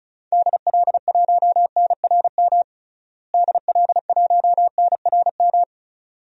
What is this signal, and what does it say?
Morse code, heard as a keyed tone, about 36 wpm, 700 Hz. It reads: DL1NRM DL1NRM
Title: DXCC-100 Call Signs - 6 Characters - Repeated 2x - 35wpm